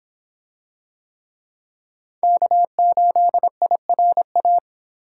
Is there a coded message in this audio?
Yes: Morse code from 2.2 to 4.6 s, heard as a keyed tone, about 26 wpm, 700 Hz. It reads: K8IRA